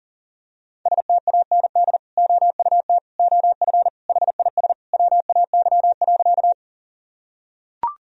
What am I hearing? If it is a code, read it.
Morse code, heard as a keyed tone, about 40 wpm, 700 Hz. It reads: STAND OUT OF HIS WAY.
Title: Sentences from Top 1000 Words - Review Part 1 - 40wpm